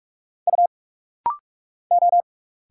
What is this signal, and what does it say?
Morse code, heard as a keyed tone, about 46 wpm, 700 Hz. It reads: U O
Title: Single Letters - Rapid-Fire 45wpm